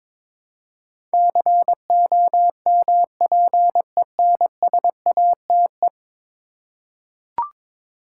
Morse code, heard as a keyed tone, about 22 wpm, 700 Hz. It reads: COMPENSATE
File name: practice / 22wpm / 10 Letter Words - First 3 Letters Given - 22wpm